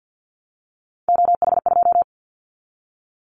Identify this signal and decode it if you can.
Morse code, heard as a keyed tone, about 50 wpm, 700 Hz. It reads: OH2
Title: Callsign Prefixes for Top 25 DXCC - Repeated 3x - 50wpm